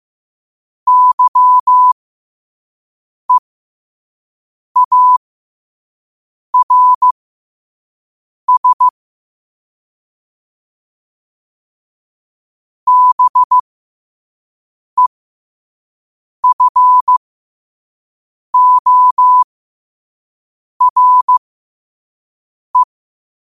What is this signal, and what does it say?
Morse code, heard as a keyed tone, about 15 wpm, spaced as for 5 wpm, 1000 Hz. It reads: YEARS BEFORE
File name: qotdfiles / 5wpm